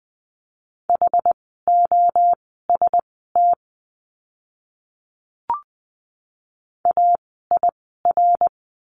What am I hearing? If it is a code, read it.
Morse code, heard as a keyed tone, about 20 wpm, spaced as for 15 wpm, 700 Hz. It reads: HOST AIR
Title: 068 Lesson 12f - Letter d - Words 20wpm